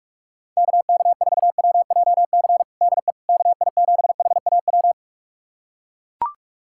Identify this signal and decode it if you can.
Morse code, heard as a keyed tone, about 41 wpm, 690 Hz. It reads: KK4WJC DE KI7HAW
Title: USA Callsign DE Callsign - 6 Characters - 45wpm